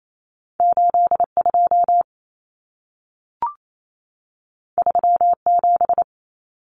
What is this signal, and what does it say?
Morse code, heard as a keyed tone, about 28 wpm, 700 Hz. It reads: 82 37
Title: Numbers - 2-Digit 28wpm